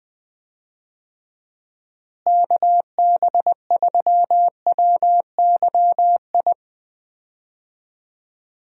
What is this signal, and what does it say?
Morse code, heard as a keyed tone, about 20 wpm, 700 Hz. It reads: KB3WYI
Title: USA Call Signs - 6 Characters - Repeated 3x - 20wpm